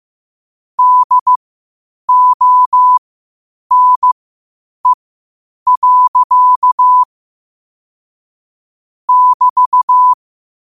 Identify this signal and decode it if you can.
Morse code, heard as a keyed tone, about 15 wpm, spaced as for 8 wpm, 1000 Hz. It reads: DONE. <BT>
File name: qotdfiles / 8wpm